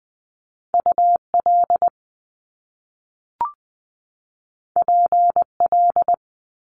Morse code, heard as a keyed tone, about 20 wpm, 700 Hz. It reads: U L P L
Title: Top Two Letter Combinations in Words 20wpm